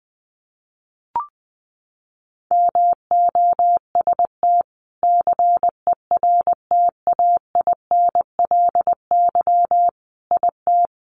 Morse code, heard as a keyed tone, about 20 wpm, 700 Hz. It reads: MOST CERTAINLY IT
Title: Sentences from Top 900 Words 20wpm